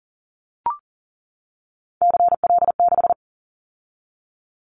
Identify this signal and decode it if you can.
Morse code, heard as a keyed tone, about 40 wpm, 700 Hz. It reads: CL6
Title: Callsign Prefixes for Top 100 DXCC - 40wpm